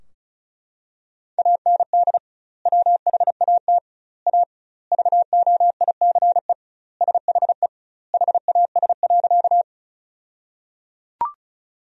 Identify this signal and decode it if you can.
Morse code, heard as a keyed tone, about 34 wpm, 700 Hz. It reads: AND WHAT A VOICE SHE HAS.
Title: 2x Word-Spacing - Sentences from Top 500 Words - 35wpm